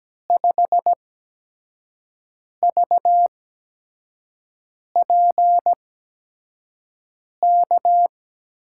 Morse code, heard as a keyed tone, about 17 wpm, spaced as for 5 wpm, 700 Hz. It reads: 5VPK
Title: USA Call Signs - 6 Characters - 8x Character Spacing - 17wpm